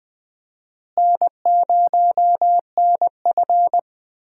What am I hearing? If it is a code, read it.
Morse code, heard as a keyed tone, about 20 wpm, 700 Hz. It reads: N0NF
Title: USA Call Signs - 4 Characters - Repeated 3x - 20wpm